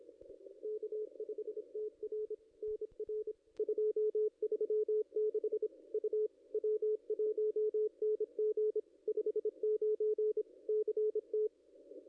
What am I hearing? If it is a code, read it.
Morse code, heard as a keyed tone, about 28 wpm, 440 Hz, with radio noise. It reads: K5TR NR 236 U W1NG 59 CT
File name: contests / ss-cw-k5tr-op-n5rz -0400